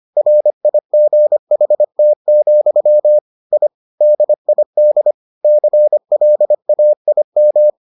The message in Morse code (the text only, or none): RIGHT, I DID CLAIM